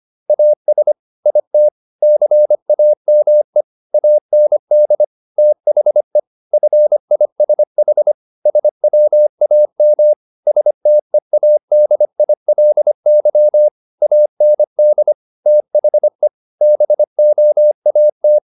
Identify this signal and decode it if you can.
Morse code, heard as a keyed tone, about 25 wpm, 600 Hz. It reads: AS IT CAME AND THE FISH SWAM STEADILY AND THE BOAT
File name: the old man and the sea part 1 25wpm